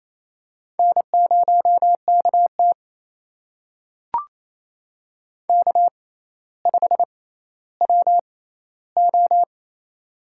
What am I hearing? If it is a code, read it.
Morse code, heard as a keyed tone, about 28 wpm, spaced as for 14 wpm, 700 Hz. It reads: N0KT K5WO
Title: USA Call Signs - 4 Characters - 6x Character Spacing - 28wpm